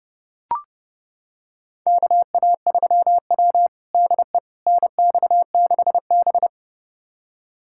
Morse code, heard as a keyed tone, about 30 wpm, 700 Hz. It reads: KA3W DE NX6B
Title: USA Callsign DE Callsign - 4 Characters - 30wpm